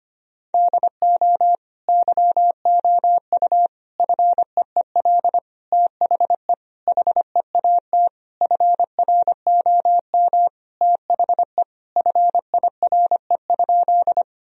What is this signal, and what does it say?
Morse code, heard as a keyed tone, about 25 wpm, 700 Hz. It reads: DO YOU FEEL THE HEAT FROM THE FIRE?
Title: Sentences from Top 1000 Words - Review Part 3 - 25wpm